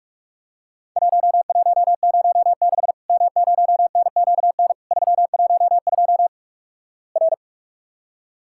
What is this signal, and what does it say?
Morse code, heard as a keyed tone, about 46 wpm, 700 Hz. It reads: 1106 M0NQN 312 R
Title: Worked All Europe DX Contest - QTC Exchange - 45wpm